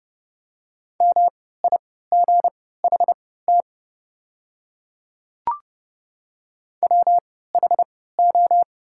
Morse code, heard as a keyed tone, about 30 wpm, spaced as for 17 wpm, 700 Hz. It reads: MIGHT WHO
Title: 3x Character Spacing - Top 300 Words - 30wpm